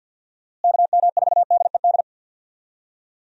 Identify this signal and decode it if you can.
Morse code, heard as a keyed tone, about 50 wpm, 700 Hz. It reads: KM4DED
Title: USA Call Signs - 6 Characters - Repeated 3x - 50wpm